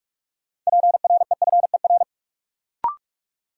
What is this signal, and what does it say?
Morse code, heard as a keyed tone, about 42 wpm, 700 Hz. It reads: PREFER
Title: Sentences from Top 1100 Words - New Word Drill - 45wpm